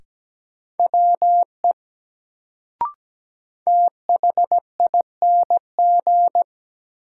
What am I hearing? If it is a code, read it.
Morse code, heard as a keyed tone, about 17 wpm, 700 Hz. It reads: WE THING
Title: Sentences from Top 100 Words - New Word Drill - 17wpm